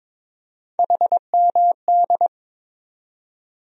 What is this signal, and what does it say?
Morse code, heard as a keyed tone, about 22 wpm, 700 Hz. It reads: HMD
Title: USA Call Sign Postfixes - 3 Letters 22wpm